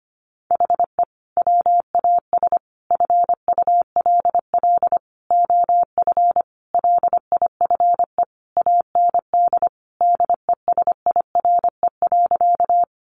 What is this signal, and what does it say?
Morse code, heard as a keyed tone, about 25 wpm, 700 Hz. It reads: HE WAS FULL OF LIFE AND DESIRE.
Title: Sentences from Top 1100 Words 25wpm